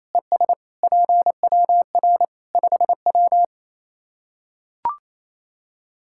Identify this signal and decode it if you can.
Morse code, heard as a keyed tone, about 27 wpm, 700 Hz. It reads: ES PWR 5W
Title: K1USN SST QSO Practice 28wpm